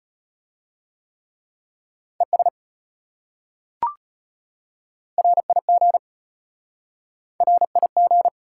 Morse code, heard as a keyed tone, about 32 wpm, 700 Hz. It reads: ES RIG RIG